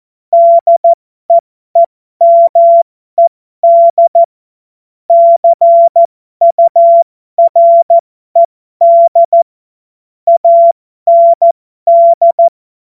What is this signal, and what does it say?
Morse code, heard as a keyed tone, about 14 wpm, spaced as for 12 wpm, 680 Hz. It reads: DEEMED CURED AND
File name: Morsekurs / Witze